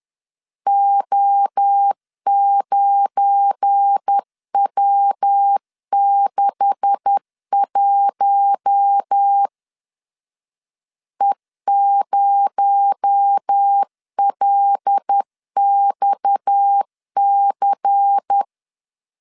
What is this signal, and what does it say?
Morse code, heard as a keyed tone, about 10 wpm, 800 Hz. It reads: O9W61 E0LXC
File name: LEZIONI / 10 SIGNS / 10B INT 10wpm 2844